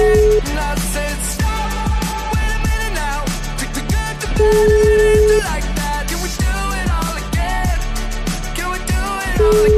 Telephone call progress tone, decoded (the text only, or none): ringback tone